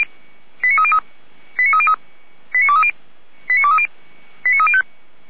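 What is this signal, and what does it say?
Five-tone selective call (ZVEI), heard as quick runs of five tones, tones of about 70 ms. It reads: 89303 89303 89230 89230 89305
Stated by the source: ZVEI five-tone call received by SDR, Feuerwehr Orlamünde (fire brigade).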